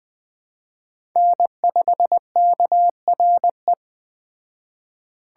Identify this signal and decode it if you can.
Morse code, heard as a keyed tone, about 20 wpm, 700 Hz. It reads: N5KRE